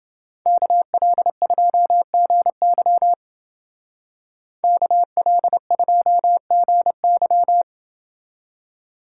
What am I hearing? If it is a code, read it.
Morse code, heard as a keyed tone, about 29 wpm, 700 Hz. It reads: KL2GY KL2GY